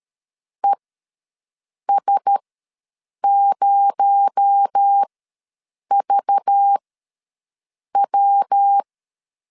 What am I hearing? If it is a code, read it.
Morse code, heard as a keyed tone, about 13 wpm, spaced as for 6 wpm, 800 Hz. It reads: ES0VW
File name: LEZIONI / 7 GF9Z / 7 GF9 ws 12wpm 3344